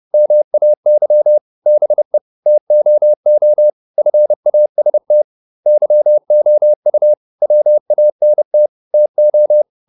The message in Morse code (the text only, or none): MAY BE TOO FAST YOU WANT TO